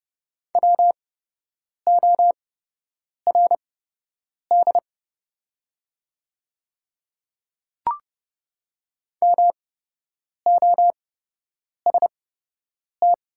Morse code, heard as a keyed tone, about 30 wpm, spaced as for 8 wpm, 700 Hz. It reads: WORD MOST
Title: 8x Character Spacing - Top 100 Words - 30wpm